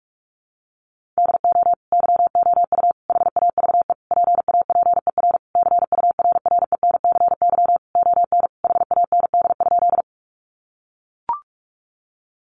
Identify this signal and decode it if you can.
Morse code, heard as a keyed tone, about 44 wpm, 700 Hz. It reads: DO YOU HAVE PAPER CURRENCY ON HAND?
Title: Sentences from Top 1500 Words 45wpm